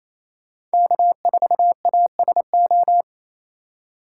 Morse code, heard as a keyed tone, about 35 wpm, 700 Hz. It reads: K4ASO
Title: Speed Racing - USA 1x3 Call Signs 28wpm